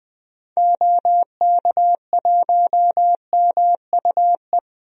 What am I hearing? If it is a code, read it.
Morse code, heard as a keyed tone, about 20 wpm, 700 Hz. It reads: OK1MUE
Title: DXCC-100 Call Signs - 6 Characters - 20wpm